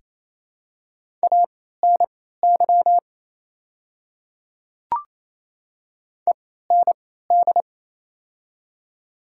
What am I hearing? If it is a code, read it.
Morse code, heard as a keyed tone, about 27 wpm, spaced as for 16 wpm, 700 Hz. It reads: ANY END
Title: 3x Character Spacing - Top 200 Words - 28wpm